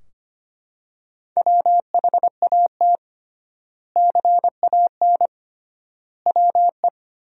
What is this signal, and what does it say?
Morse code, heard as a keyed tone, about 25 wpm, 700 Hz. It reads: WHAT CAN WE
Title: 3x Word-Spacing - Sentences from Top 100 Words - 25wpm